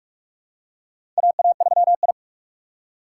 Morse code, heard as a keyed tone, about 46 wpm, 700 Hz. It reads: AA3I
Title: USA Call Signs - 4 Characters - 4x Character Spacing - 45wpm